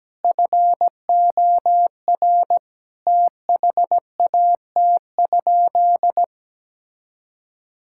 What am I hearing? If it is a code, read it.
Morse code, heard as a keyed tone, about 17 wpm, 700 Hz. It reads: FOR THAT?
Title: Sentences from Top 1000 Words - Review Part 6 - 17wpm